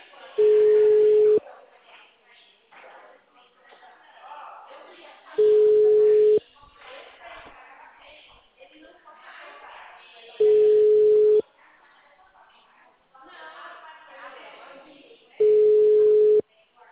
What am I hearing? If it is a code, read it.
Ringback tone (Europe, 425 Hz), with speech.
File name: out- -2044- - - .49950